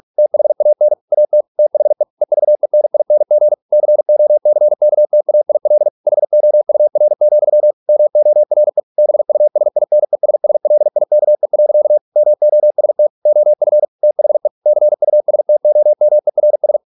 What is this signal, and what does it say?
Morse code, heard as a keyed tone, about 46 wpm, 600 Hz. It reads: THAN AT THE EVENING COCKTAIL HOUR, MORE BUSINESSLIKE. MOST OF THE CUSTOMERS